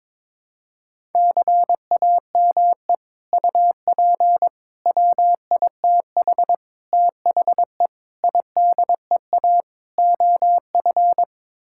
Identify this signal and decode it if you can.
Morse code, heard as a keyed tone, about 22 wpm, 700 Hz. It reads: CAME UP WITH THE IDEA OF
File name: practice / 22wpm / Sets of 6 Words - 22wpm